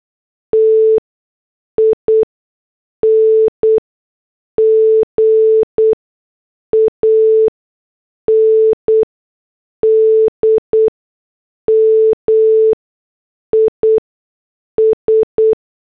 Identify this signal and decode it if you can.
Morse code, heard as a keyed tone, about 8 wpm, spaced as for 6 wpm, 440 Hz. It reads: TING AND MIS